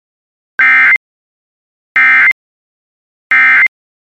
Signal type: Emergency Alert System (SAME) tones